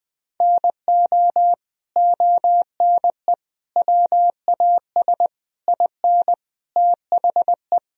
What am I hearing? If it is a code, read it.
Morse code, heard as a keyed tone, about 21 wpm, 700 Hz. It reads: NO ONE WAS IN THE
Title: Sentences from Top 1000 Words - Review Part 4 - 20wpm